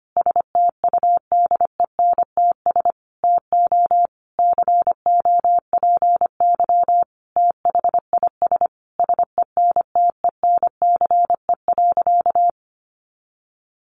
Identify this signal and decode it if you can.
Morse code, heard as a keyed tone, about 25 wpm, 700 Hz. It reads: STUDENTS TO COPY THIS SENTENCE.
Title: Additional Easy Phrases 25wpm